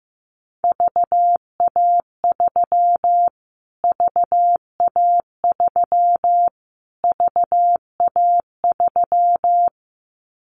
Morse code, heard as a keyed tone, about 15 wpm, 700 Hz. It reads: VA3 VA3 VA3
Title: Callsign Prefixes for Top 25 DXCC - Repeated 3x - 15wpm